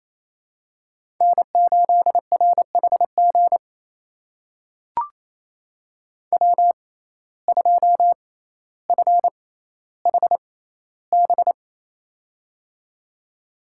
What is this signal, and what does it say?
Morse code, heard as a keyed tone, about 28 wpm, spaced as for 14 wpm, 700 Hz. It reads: N8RHG W2FHB
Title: USA Call Signs - 5 Characters - 6x Character Spacing - 28wpm